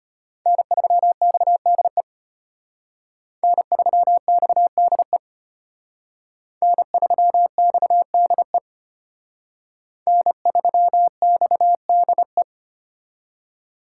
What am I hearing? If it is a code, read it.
Morse code, heard as a keyed tone, about 32 wpm, 700 Hz. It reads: N3XDE N3XDE N3XDE N3XDE